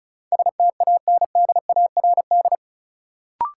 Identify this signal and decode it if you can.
Morse code, heard as a keyed tone, about 36 wpm, 700 Hz. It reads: STANDARD
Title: Sentences from Top 1000 Words - Review Part 6 - New Word Drill - 35wpm